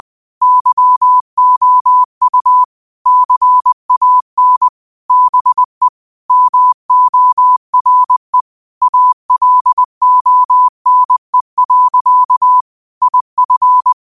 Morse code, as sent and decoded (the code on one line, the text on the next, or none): -.-- --- ..- / -.-. .- -. / -... / -- --- .-. / .- .-.. --- -. .-.-.- / .. ..-.
YOU CAN BE MORE ALONE. IF